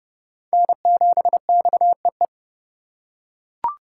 Morse code, heard as a keyed tone, about 29 wpm, 700 Hz. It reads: N7XEE